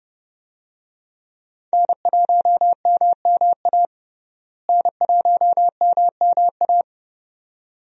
Morse code, heard as a keyed tone, about 30 wpm, 700 Hz. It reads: N1MMA N1MMA
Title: USA Call Signs - 5 Characters - Repeated 2x - 30wpm